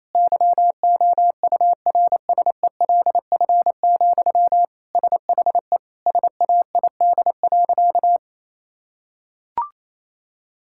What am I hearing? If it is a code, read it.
Morse code, heard as a keyed tone, about 28 wpm, 700 Hz. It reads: YOURSELF, SHE SAID.
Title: Sentences from Top 1600 Words 28wpm